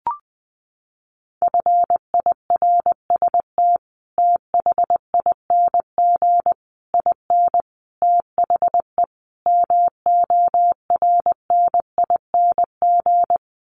Morse code, heard as a keyed tone, about 20 wpm, 700 Hz. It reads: FIRST THING IN THE MORNING
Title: Sets of 5 Words Encore2 20wpm